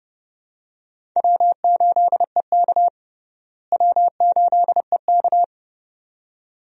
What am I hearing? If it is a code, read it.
Morse code, heard as a keyed tone, about 30 wpm, 700 Hz. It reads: W8EK W8EK
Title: USA Call Signs - 4 Characters - Repeated 2x - 30wpm